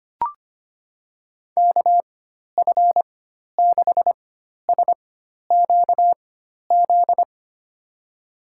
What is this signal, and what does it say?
Morse code, heard as a keyed tone, about 25 wpm, spaced as for 12 wpm, 700 Hz. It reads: KF6SQZ
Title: USA Call Signs - 6 Characters - 4x Character Spacing - 25wpm